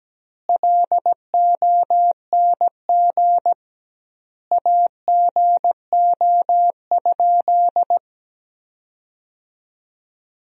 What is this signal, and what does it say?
Morse code, heard as a keyed tone, about 17 wpm, 700 Hz. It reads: LONG AGO?
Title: 2x Word-Spacing - Sentences from Top 500 Words - 17wpm